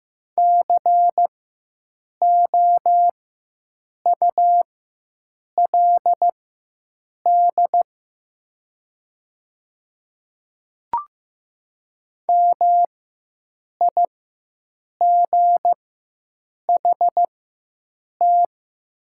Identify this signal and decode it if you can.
Morse code, heard as a keyed tone, about 15 wpm, spaced as for 7 wpm, 700 Hz. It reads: COULD MIGHT
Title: 4x Character Spacing - Top 500 Words - 15wpm